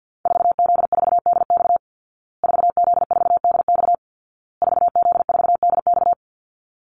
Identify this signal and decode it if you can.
Morse code, heard as a keyed tone, about 50 wpm, 700 Hz. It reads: 4Z4DX 4Z4DX 4Z4DX
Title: Call Signs Repeated 2x - 50wpm